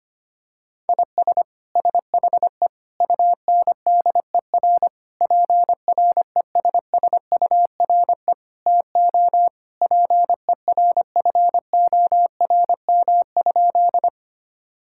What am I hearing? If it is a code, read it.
Morse code, heard as a keyed tone, about 25 wpm, 700 Hz. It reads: IS SHE UNDER PRESSURE TO PERFORM?
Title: Sentences from Top 1000 Words - Review Part 6 - 25wpm